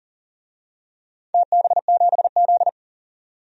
Morse code, heard as a keyed tone, about 40 wpm, 700 Hz. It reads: TB7Z
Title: DXCC-100 Call Signs - 4 Characters - 40wpm